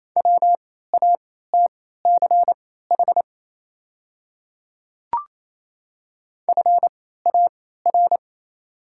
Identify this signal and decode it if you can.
Morse code, heard as a keyed tone, about 28 wpm, spaced as for 16 wpm, 700 Hz. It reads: WATCH FAR
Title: 3x Character Spacing - Top 300 Words - 28wpm